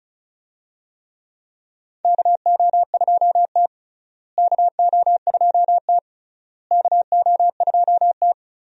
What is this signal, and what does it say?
Morse code, heard as a keyed tone, about 36 wpm, 700 Hz. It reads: KO2T KO2T KO2T
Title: USA Call Signs - 4 Characters - Repeated 3x - 35wpm